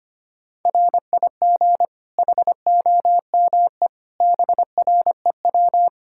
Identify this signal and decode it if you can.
Morse code, heard as a keyed tone, about 25 wpm, 700 Hz. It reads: RIG HOME BREW
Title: Practice QSO - Exchange 2 - Info - 25wpm